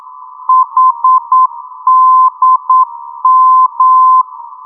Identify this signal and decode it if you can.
Morse code, heard as a keyed tone, about 9 wpm, 1030 Hz. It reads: HDM
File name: europe / d-8023 - ndb coleman hdm-334 (2004) (rainer brannolte, d) xc